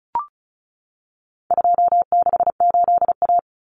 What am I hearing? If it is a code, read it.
Morse code, heard as a keyed tone, about 34 wpm, 700 Hz. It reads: 268A
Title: Common Cut Numbers - 4-Digit 35wpm